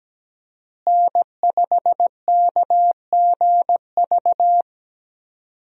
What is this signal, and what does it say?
Morse code, heard as a keyed tone, about 17 wpm, 700 Hz. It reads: N5KGV
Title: USA Call Signs - 5 Characters - Repeated 2x - 17wpm